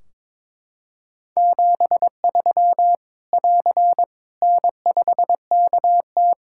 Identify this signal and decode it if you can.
Morse code, heard as a keyed tone, about 22 wpm, 700 Hz. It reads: 73 <AR> N5KT
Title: K1USN SST QSO Practice 22wpm